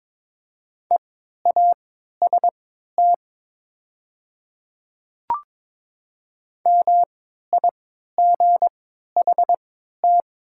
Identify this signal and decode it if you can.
Morse code, heard as a keyed tone, about 22 wpm, spaced as for 12 wpm, 700 Hz. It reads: EAST MIGHT
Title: 3x Character Spacing - Top 500 Words - 22wpm